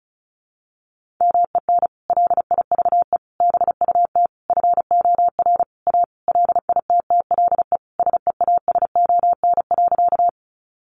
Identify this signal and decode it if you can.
Morse code, heard as a keyed tone, about 34 wpm, 700 Hz. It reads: MEN LIVE BUT FOR A LITTLE SEASON.
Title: Sentences from Top 1400 Words 35wpm